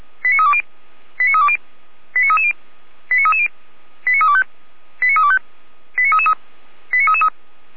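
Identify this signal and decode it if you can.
Five-tone selective call (ZVEI), heard as quick runs of five tones, tones of about 80 ms. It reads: 89320 89320 89330 89330 89325 89325 89333 89333
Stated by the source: ZVEI five-tone selective call received on an SDR, archived by Feuerwehr Orlamünde (fire brigade).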